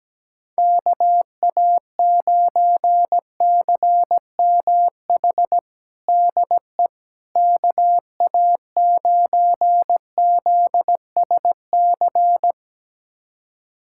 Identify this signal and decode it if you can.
Morse code, heard as a keyed tone, about 17 wpm, 700 Hz. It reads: KA9CMH DE KA9ZSC
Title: USA Callsign DE Callsign 17wpm